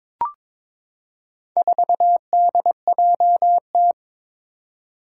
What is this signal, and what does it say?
Morse code, heard as a keyed tone, about 22 wpm, 700 Hz. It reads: 4DJT
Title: Four Challenging Letter-Number Combinations 22wpm